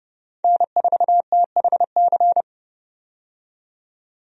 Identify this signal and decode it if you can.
Morse code, heard as a keyed tone, about 32 wpm, 700 Hz. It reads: N4THC
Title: USA Call Signs - 5 Characters - 30wpm